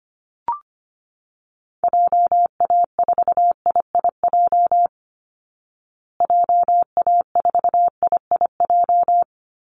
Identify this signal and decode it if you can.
Morse code, heard as a keyed tone, about 25 wpm, 700 Hz. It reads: JA4IIJ JA4IIJ
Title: Call Signs Repeated 3x - 25wpm